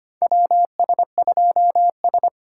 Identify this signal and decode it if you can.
Morse code, heard as a keyed tone, about 25 wpm, 700 Hz. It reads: WS2S
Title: USA Call Signs - 4 Characters - 3x Character Spacing - 25wpm